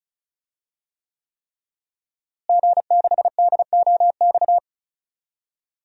Morse code, heard as a keyed tone, about 34 wpm, 700 Hz. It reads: G6DOX
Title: DXCC-100 Call Signs - 5 Characters - Repeated 3x - 35wpm